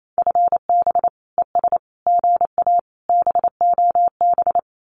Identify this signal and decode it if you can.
Morse code, heard as a keyed tone, about 28 wpm, 700 Hz. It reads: FB ES GA BOB
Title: Common QSO Phrases Encore 28wpm